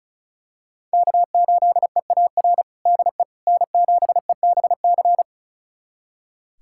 Morse code, heard as a keyed tone, about 36 wpm, 700 Hz. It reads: K8EAR DE N7EBC
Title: USA Callsign DE Callsign - 5 Characters - 35wpm